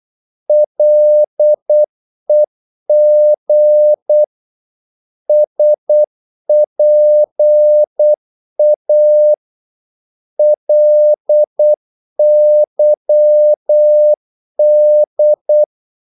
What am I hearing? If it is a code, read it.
Morse code, heard as a keyed tone, about 8 wpm, 600 Hz. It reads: LEG SPA LYD
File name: Korte ord DK 8wpm